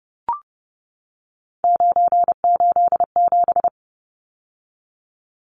Morse code, heard as a keyed tone, about 30 wpm, 700 Hz. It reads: 987
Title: Numbers - 3-Digit 30wpm